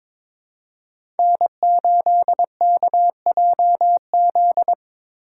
Morse code, heard as a keyed tone, about 22 wpm, 700 Hz. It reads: N8KJZ
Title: Code Groups - 5 Letter:Numbers - 22wpm